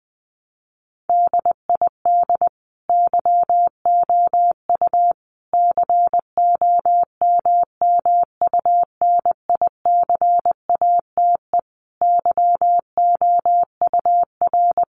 Morse code, heard as a keyed tone, about 19 wpm, 700 Hz. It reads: DID YOU COMMUNICATE YOUR
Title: Sentences from Top 1300 Words 20wpm